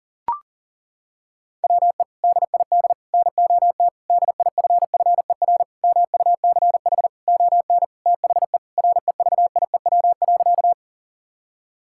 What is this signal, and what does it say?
Morse code, heard as a keyed tone, about 40 wpm, 700 Hz. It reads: WE DID NOT DIFFER MUCH ON THE REVIEW.
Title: Sentences from Top 1000 Words - Review Part 6 - 40wpm